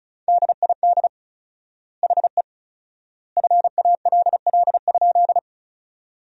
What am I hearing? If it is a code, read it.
Morse code, heard as a keyed tone, about 36 wpm, 700 Hz. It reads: DID HE FALL?
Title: 4x Word-Spacing - Sentences from Top 500 Words - 35wpm